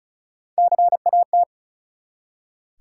Morse code, heard as a keyed tone, about 34 wpm, 700 Hz. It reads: CAT
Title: Three Letter Words - 35wpm